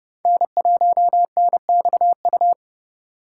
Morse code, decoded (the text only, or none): N1NXU